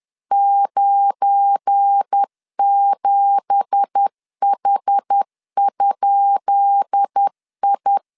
Morse code, heard as a keyed tone, about 10 wpm, 800 Hz. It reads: 97H?I
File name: LEZIONI / 10 SIGNS / 10B INT 10wpm 2844